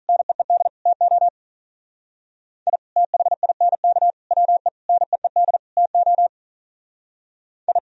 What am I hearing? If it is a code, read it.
Morse code, heard as a keyed tone, about 43 wpm, 700 Hz. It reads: NEED TO I THINK WE NEED TO I
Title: Speed-Racing - Sets of 5 Words 35wpm